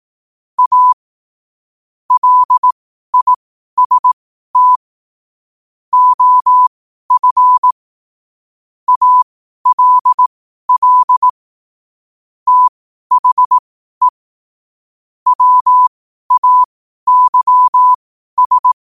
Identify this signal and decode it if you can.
Morse code, heard as a keyed tone, about 18 wpm, spaced as for 12 wpm, 1000 Hz. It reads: A LIST OF ALL THE WAYS
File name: qotdfiles / 12wpm